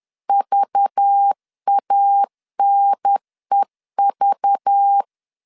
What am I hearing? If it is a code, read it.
Morse code, heard as a keyed tone, about 10 wpm, 800 Hz. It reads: VANEV
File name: CW / DET / 1 9 1500C S B 10wpm 4439